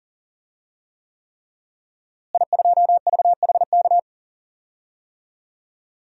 Morse code, heard as a keyed tone, about 40 wpm, 700 Hz. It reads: I2VHK
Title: DXCC-100 Call Signs - 5 Characters - Repeated 3x - 40wpm